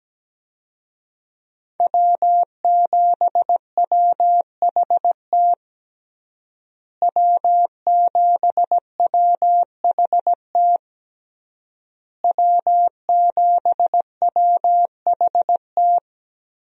Morse code, heard as a keyed tone, about 17 wpm, 700 Hz. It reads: W7WHT W7WHT W7WHT
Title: USA Call Signs - 5 Characters - Repeated 3x - 17wpm